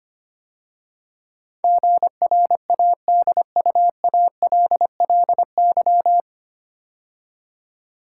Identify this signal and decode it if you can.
Morse code, heard as a keyed tone, about 25 wpm, 700 Hz. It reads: GRADUALLY
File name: practice / 25wpm / 9 Letter Words - First 3 Letters Given - 25wpm